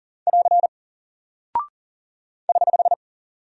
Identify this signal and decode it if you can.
Morse code, heard as a keyed tone, about 40 wpm, 700 Hz. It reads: <AR> <HH>
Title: Single Character (Letter:Number:Punctuation:Prosign) Alt 40wpm